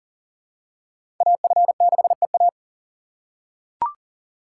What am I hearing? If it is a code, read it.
Morse code, heard as a keyed tone, about 37 wpm, 700 Hz. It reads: AF6EA
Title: USA Call Signs - 5 Characters - 3x Character Spacing - 40wpm